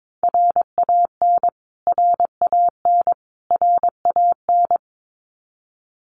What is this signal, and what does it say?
Morse code, heard as a keyed tone, about 22 wpm, 700 Hz. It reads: RAN RAN RAN
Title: Top 500 Words Repeated 22wpm